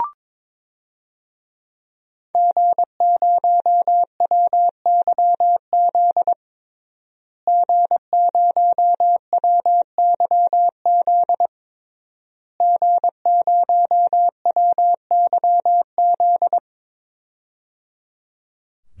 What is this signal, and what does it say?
Morse code, heard as a keyed tone, about 22 wpm, 700 Hz. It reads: G0WYZ G0WYZ G0WYZ
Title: DXCC-100 Call Signs - 5 Characters - Repeated 3x - 22wpm